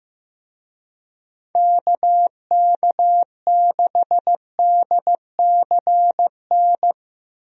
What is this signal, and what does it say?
Morse code, heard as a keyed tone, about 15 wpm, 700 Hz. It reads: KK6DCN
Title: USA Call Signs - 6 Characters - Repeated 3x - 15wpm